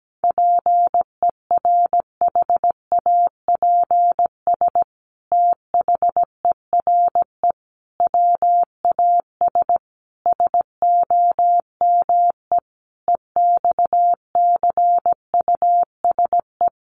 Morse code, heard as a keyed tone, about 17 wpm, 700 Hz. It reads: PERHAPS THERE WAS SOME EXCUSE
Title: Sentences from Top 700 Words 17wpm